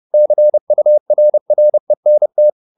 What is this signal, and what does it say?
Morse code, heard as a keyed tone, about 30 wpm, 600 Hz. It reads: CURRENT
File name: the old man and the sea / the old man and the sea 1 30wpm